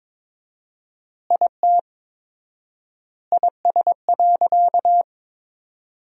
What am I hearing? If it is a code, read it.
Morse code, heard as a keyed tone, about 22 wpm, 700 Hz. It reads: IT IS.
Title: 4x Word-Spacing - Sentences from Top 200 Words - 22wpm